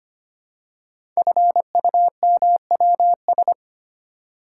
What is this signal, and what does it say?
Morse code, heard as a keyed tone, about 25 wpm, 700 Hz. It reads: FUMWS
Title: Code Groups - 5 Letters - 25wpm